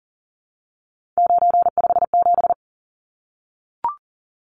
Morse code, heard as a keyed tone, about 40 wpm, 700 Hz. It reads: 957
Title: Numbers - 3-Digit 40wpm